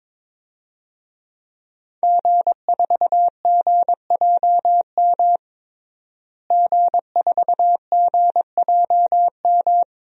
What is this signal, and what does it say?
Morse code, heard as a keyed tone, about 22 wpm, 700 Hz. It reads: G4GJM G4GJM
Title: DXCC-100 Call Signs - 5 Characters - Repeated 2x - 22wpm